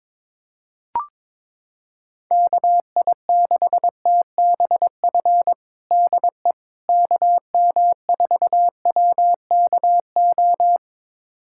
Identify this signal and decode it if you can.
Morse code, heard as a keyed tone, about 22 wpm, 700 Hz. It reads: KI6TBF DE KM4WKO
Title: USA Callsign DE Callsign 22wpm